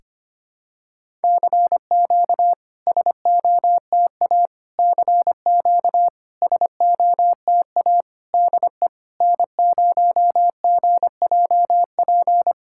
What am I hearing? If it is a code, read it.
Morse code, heard as a keyed tone, about 25 wpm, 690 Hz. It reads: CQ SOTA CQ SOTA DE N0GJP
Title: SOTA QSO Practice 25wpm